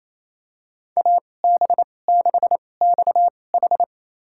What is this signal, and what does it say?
Morse code, heard as a keyed tone, about 28 wpm, spaced as for 20 wpm, 700 Hz. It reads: AB6XH
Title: USA Call Signs - 5 Characters - 2x Character Spacing - 28wpm